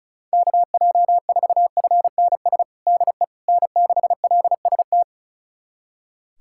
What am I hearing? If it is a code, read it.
Morse code, heard as a keyed tone, about 36 wpm, 700 Hz. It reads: KJ4FNS DE N6LST
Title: USA Callsign DE Callsign 35wpm